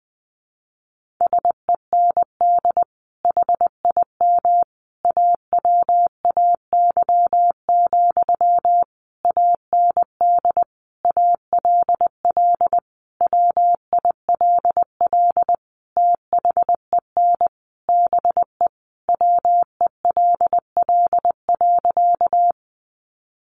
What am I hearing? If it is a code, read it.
Morse code, heard as a keyed tone, about 20 wpm, 700 Hz. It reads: SEND HIM AWAY, AND ALL WILL THEN BE WELL.
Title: Sentences from Top 1500 Words 20wpm